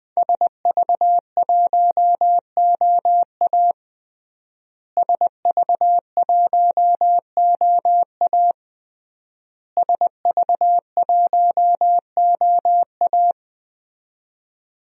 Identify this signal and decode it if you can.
Morse code, heard as a keyed tone, about 21 wpm, 700 Hz. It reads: SV1OA SV1OA SV1OA
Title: DXCC-100 Call Signs - 5 Characters - Repeated 3x - 20wpm